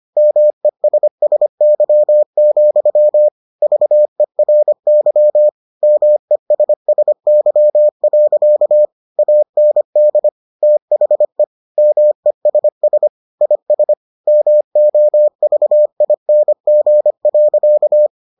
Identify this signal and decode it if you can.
Morse code, heard as a keyed tone, about 25 wpm, 600 Hz. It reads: MESSY, VERY MESSY. AND THE MESS IS MOVING.